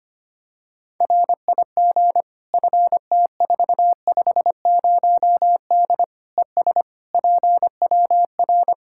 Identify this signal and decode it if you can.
Morse code, heard as a keyed tone, about 25 wpm, 700 Hz. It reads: RIG FT450D ES PWR
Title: Practice QSO - Exchange 2 - Info - 25wpm